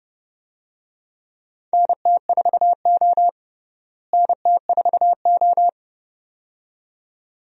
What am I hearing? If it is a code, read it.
Morse code, heard as a keyed tone, about 30 wpm, 700 Hz. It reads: NT4O NT4O
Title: USA Call Signs - 4 Characters - Repeated 2x - 30wpm